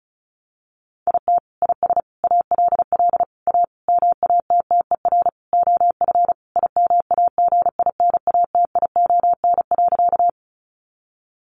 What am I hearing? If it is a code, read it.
Morse code, heard as a keyed tone, about 34 wpm, 700 Hz. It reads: IT IS ALL A MATTER OF IMAGINATION.
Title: Sentences from Top 1300 Words 35wpm